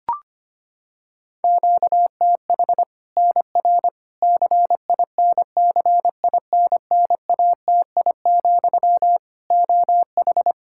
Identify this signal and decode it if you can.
Morse code, heard as a keyed tone, about 25 wpm, 700 Hz. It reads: QTH NR CINCINNATI, OH
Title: Practice QSO - Exchange 1 - Beginning - 25wpm